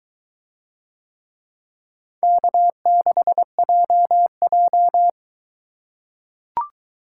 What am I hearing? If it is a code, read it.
Morse code, heard as a keyed tone, about 22 wpm, 700 Hz. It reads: K6JJ